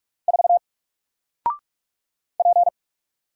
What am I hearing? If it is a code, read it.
Morse code, heard as a keyed tone, about 44 wpm, 700 Hz. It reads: 4 P
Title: Single Letter-Number Alt 45wpm